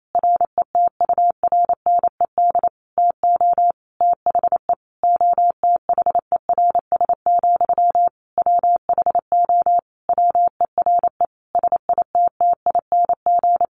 Morse code, heard as a keyed tone, about 28 wpm, 700 Hz. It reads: RETURNED TO THE OTHERS, WHO WERE SITTING
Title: A Sound Of Thunder 28wpm